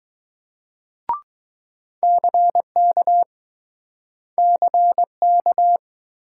Morse code, heard as a keyed tone, about 20 wpm, 700 Hz. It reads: CK CK